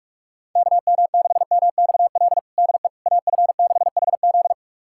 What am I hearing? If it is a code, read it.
Morse code, heard as a keyed tone, about 46 wpm, 690 Hz. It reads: KM6MXL DE AF6HZ
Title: USA Callsign DE Callsign 45wpm